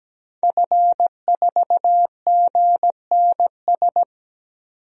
Morse code, heard as a keyed tone, about 17 wpm, 700 Hz. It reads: F4GNS
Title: DXCC-100 Call Signs - 5 Characters - 17wpm